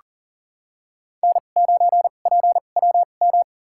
Morse code, heard as a keyed tone, about 40 wpm, spaced as for 29 wpm, 700 Hz. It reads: N9PWM
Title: USA Call Signs - 5 Characters - 2x Character Spacing - 40wpm